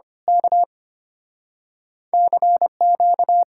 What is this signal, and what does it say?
Morse code, heard as a keyed tone, about 29 wpm, 700 Hz. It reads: K CQ
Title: Speed-Racing Common QSO Phrases - 25wpm